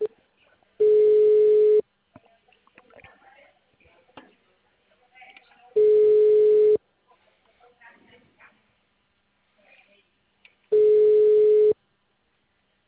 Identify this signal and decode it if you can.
Ringback tone (Europe, 425 Hz), with speech.
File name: out- -2029